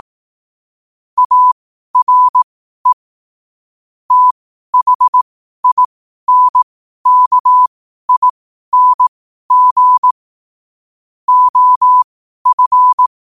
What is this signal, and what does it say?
Morse code, heard as a keyed tone, about 18 wpm, spaced as for 12 wpm, 1000 Hz. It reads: ARE THINKING OF
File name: qotdfiles / 12wpm